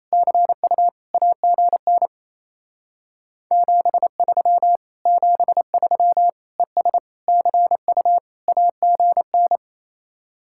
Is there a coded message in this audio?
Yes: Morse code from 0.1 to 9.6 s, heard as a keyed tone, about 37 wpm, 700 Hz. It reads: CU AGN 73 73 ES CU AGN